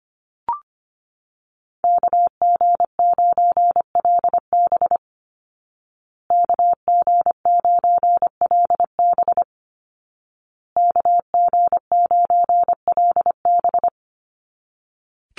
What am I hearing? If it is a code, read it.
Morse code, heard as a keyed tone, about 25 wpm, 700 Hz. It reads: KG9LB KG9LB KG9LB
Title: Call Signs Repeated 2x - 25wpm